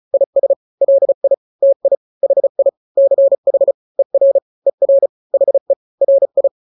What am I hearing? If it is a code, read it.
Morse code, heard as a keyed tone, about 34 wpm, 550 Hz. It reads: IS LI TI HI CH ER ER HE RI
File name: n-grams / 2-grams 35 30 WPM